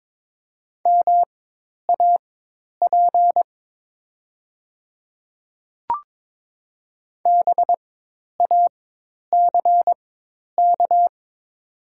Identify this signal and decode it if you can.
Morse code, heard as a keyed tone, about 22 wpm, spaced as for 10 wpm, 700 Hz. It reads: MAP BACK
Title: 4x Character Spacing - Top 500 Words - 22wpm